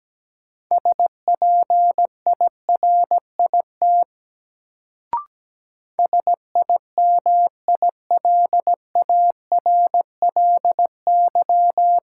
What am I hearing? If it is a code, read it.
Morse code, heard as a keyed tone, about 17 wpm, 700 Hz. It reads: SPIRIT SIMILARLY
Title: Sentences from Top 1400 Words - New Word Drill - 17wpm